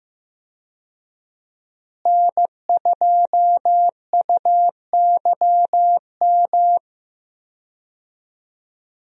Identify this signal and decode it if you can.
Morse code, heard as a keyed tone, about 15 wpm, 700 Hz. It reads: N2UYM
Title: USA Call Signs - 5 Characters - Repeated 2x - 15wpm